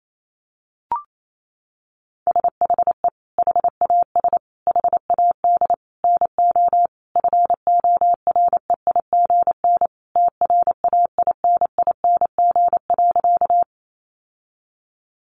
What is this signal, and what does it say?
Morse code, heard as a keyed tone, about 28 wpm, 700 Hz. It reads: SHE HAS HAD NO FOREIGN TRAINING.
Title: Sentences from Top 1400 Words 28wpm